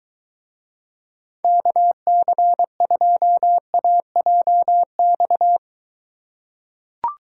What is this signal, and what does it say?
Morse code, heard as a keyed tone, about 23 wpm, 700 Hz. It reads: KC2AJX